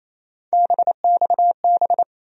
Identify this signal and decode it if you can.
Morse code, heard as a keyed tone, about 28 wpm, 700 Hz. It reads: BXB
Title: USA Call Sign Postfixes - 3 Letters 28wpm